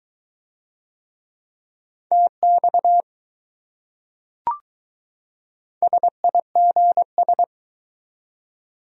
Morse code, heard as a keyed tone, about 20 wpm, 700 Hz. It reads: TX SIGS